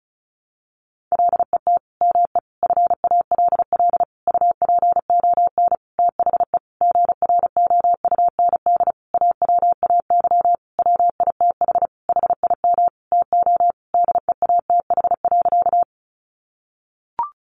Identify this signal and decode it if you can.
Morse code, heard as a keyed tone, about 34 wpm, 700 Hz. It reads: LET ME FALL UPON THE GROUND AWAY WITH HIM TO DEATH.
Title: Sentences from Top 1100 Words 35wpm